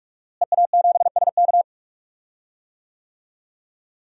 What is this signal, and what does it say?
Morse code, heard as a keyed tone, about 41 wpm, 700 Hz. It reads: EA7SK
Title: DXCC-100 Call Signs - 5 Characters - Repeated 3x - 45wpm